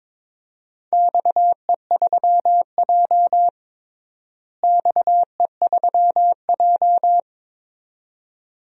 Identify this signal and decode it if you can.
Morse code, heard as a keyed tone, about 22 wpm, 700 Hz. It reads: XE3J XE3J
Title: DXCC-100 Call Signs - 4 Characters - Repeated 2x - 22wpm